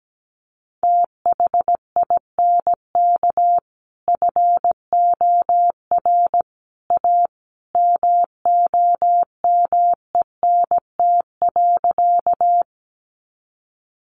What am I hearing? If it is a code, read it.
Morse code, heard as a keyed tone, about 17 wpm, 700 Hz. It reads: THINK FOR A MOMENT.
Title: Sentences from Top 1000 Words 17wpm